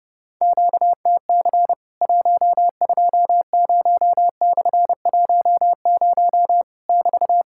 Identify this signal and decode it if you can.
Morse code, heard as a keyed tone, about 30 wpm, 700 Hz. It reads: QTC 120/10 <BT>
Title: Worked All Europe DX Contest - QTC Exchange - 30wpm